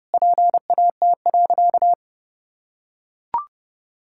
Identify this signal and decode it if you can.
Morse code, heard as a keyed tone, about 32 wpm, 700 Hz. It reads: PAT.
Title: Pangram Challenge v2 - 30wpm